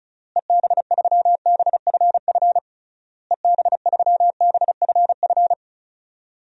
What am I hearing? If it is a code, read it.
Morse code, heard as a keyed tone, about 36 wpm, 700 Hz. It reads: EB3BFF EB3BFF
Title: DXCC-100 Call Signs - 6 Characters - Repeated 3x - 35wpm